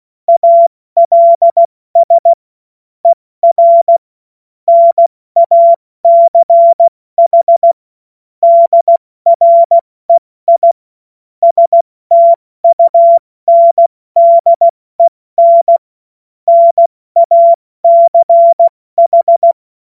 Morse code, heard as a keyed tone, about 16 wpm, 680 Hz. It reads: ALS ER NACH DREI STUNDEN NACH
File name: Witz-16-September-2025-70-F80